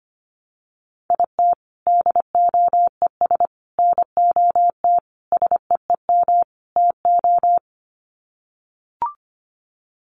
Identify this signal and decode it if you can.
Morse code, heard as a keyed tone, about 25 wpm, 700 Hz. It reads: IT DOES NOT SEEM TO
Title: Sets of 5 Words Encore2 25wpm